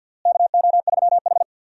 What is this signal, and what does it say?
Morse code, heard as a keyed tone, about 46 wpm, 700 Hz. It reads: KO3H
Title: USA Call Signs - 4 Characters - Repeated 3x - 50wpm